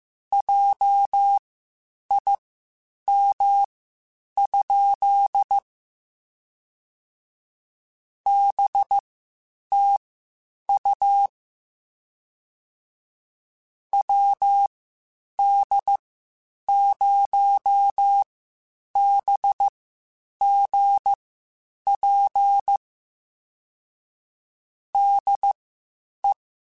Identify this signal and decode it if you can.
Morse code, heard as a keyed tone, about 15 wpm, spaced as for 8 wpm, 800 Hz. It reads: JIM? BTU WD0BGP DE
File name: (spaced) / QSOs Spaced-mp3s / WD0BGP 1 @13 sp